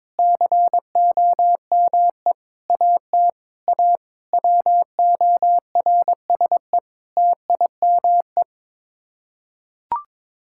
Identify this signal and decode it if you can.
Morse code, heard as a keyed tone, about 22 wpm, 700 Hz. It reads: COME AT A WORSE TIME
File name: practice / 22wpm / Sets of 8 Words - 22wpm